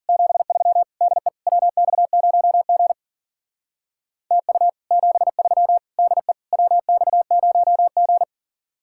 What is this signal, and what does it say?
Morse code, heard as a keyed tone, about 50 wpm, 700 Hz. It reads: 73 DE WX0G TU 73 DE WX0G
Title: Speed-Racing Common QSO Phrases Encore - 40wpm